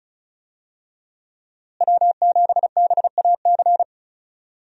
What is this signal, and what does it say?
Morse code, heard as a keyed tone, about 34 wpm, 700 Hz. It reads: W7BAC
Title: USA Call Signs - 5 Characters - Repeated 3x - 35wpm